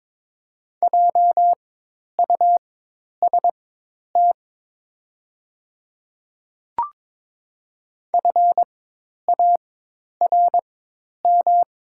Morse code, heard as a keyed tone, about 22 wpm, spaced as for 10 wpm, 700 Hz. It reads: JUST FARM
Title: 4x Character Spacing - Top 500 Words - 22wpm